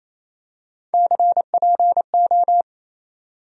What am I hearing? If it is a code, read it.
Morse code, heard as a keyed tone, about 27 wpm, 700 Hz. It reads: CPO
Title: USA Call Sign Postfixes - 3 Letters 28wpm